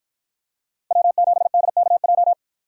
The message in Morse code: W7DKJ

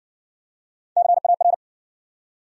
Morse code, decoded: CAR

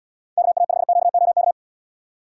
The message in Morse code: KA5QOK

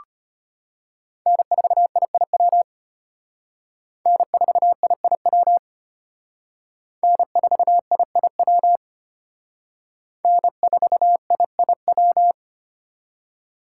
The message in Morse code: N4IIW N4IIW N4IIW N4IIW

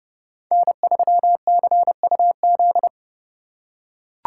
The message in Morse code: N3CUZ

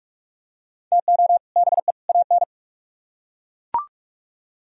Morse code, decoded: TO BE AN

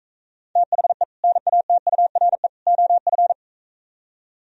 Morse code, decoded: THE NATURE OF

